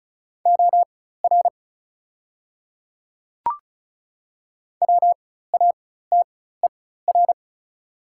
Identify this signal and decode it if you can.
Morse code, heard as a keyed tone, about 35 wpm, spaced as for 16 wpm, 700 Hz. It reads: OR WATER
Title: 4x Character Spacing - Top 100 Words - 35wpm